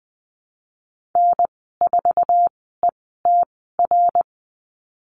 Morse code, heard as a keyed tone, about 20 wpm, spaced as for 14 wpm, 700 Hz. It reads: N4ETR
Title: 076 Lesson 13g - Letter L - Call Signs 20wpm